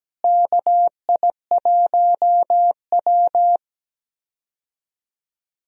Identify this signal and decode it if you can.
Morse code, heard as a keyed tone, about 17 wpm, 700 Hz. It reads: KI1W
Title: USA Call Signs - 4 Characters - Repeated 3x - 17wpm